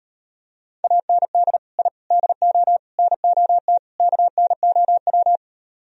Morse code, heard as a keyed tone, about 38 wpm, 700 Hz. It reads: AND I DO NOT KNOW